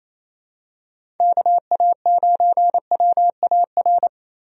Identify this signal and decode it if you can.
Morse code, heard as a keyed tone, about 27 wpm, 700 Hz. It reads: KA9WAR
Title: USA Call Signs - 6 Characters - Repeated 2x - 28wpm